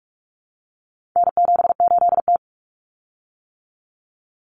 Morse code, heard as a keyed tone, about 44 wpm, 700 Hz. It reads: N78T
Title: Common Cut Numbers - 4-Digit 45wpm